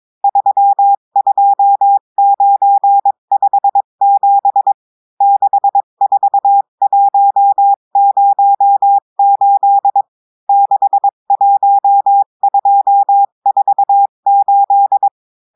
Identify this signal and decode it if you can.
Morse code, heard as a keyed tone, about 22 wpm, 800 Hz. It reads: 32957 64108 61248
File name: cours-du-mercredi / audios / cours-853-du-2-mai-2024-etxte chiffre et texte clair-a-20-wpm